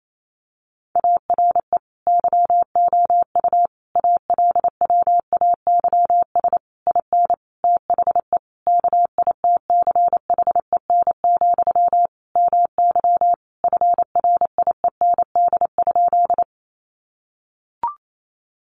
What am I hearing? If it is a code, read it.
Morse code, heard as a keyed tone, about 28 wpm, 700 Hz. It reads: ARE YOU ALWAYS IN THE KITCHEN, MY FRIEND?
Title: Sentences from Top 1400 Words 28wpm